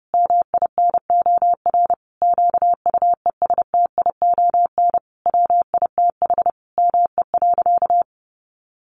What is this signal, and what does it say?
Morse code, heard as a keyed tone, about 30 wpm, 700 Hz. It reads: MINOR QUESTION WITH ME.
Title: Sentences from Top 1500 Words 30wpm